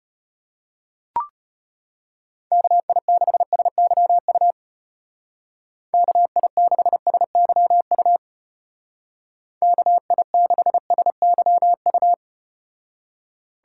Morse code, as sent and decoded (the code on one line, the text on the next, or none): -.- .. -.... ... -.-- ..- / -.- .. -.... ... -.-- ..- / -.- .. -.... ... -.-- ..-
KI6SYU KI6SYU KI6SYU